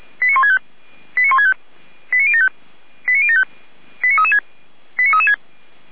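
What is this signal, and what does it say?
Five-tone selective call (ZVEI), heard as quick runs of five tones, tones of about 80 ms. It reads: 89165 89165 89075 89075 89336 89336
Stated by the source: Received on an SDR.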